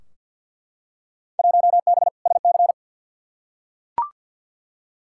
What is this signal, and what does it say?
Morse code, heard as a keyed tone, about 51 wpm, 700 Hz. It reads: 1B SC
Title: Field Day Exchanges Encore - 50wpm